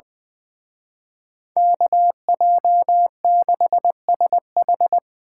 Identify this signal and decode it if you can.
Morse code, heard as a keyed tone, about 24 wpm, 700 Hz. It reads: KJ6SH
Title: Speed Racing - USA 2x2 Call Signs 20wpm